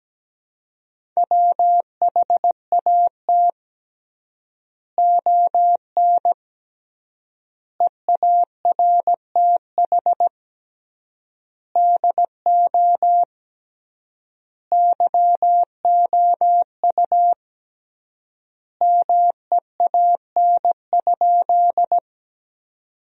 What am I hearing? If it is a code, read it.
Morse code, heard as a keyed tone, about 17 wpm, 700 Hz. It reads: WHAT ON EARTH DO YOU MEAN?
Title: 3x Word-Spacing - Sentences from Top 200 Words - 17wpm